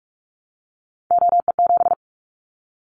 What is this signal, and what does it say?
Morse code, heard as a keyed tone, about 44 wpm, 700 Hz. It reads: OE7
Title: Callsign Prefixes for Top 100 DXCC - 45wpm